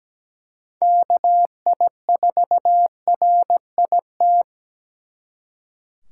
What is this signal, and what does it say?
Morse code, heard as a keyed tone, about 20 wpm, 700 Hz. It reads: KI4RIT
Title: Speed Racing - USA 2x3 Call Signs 17wpm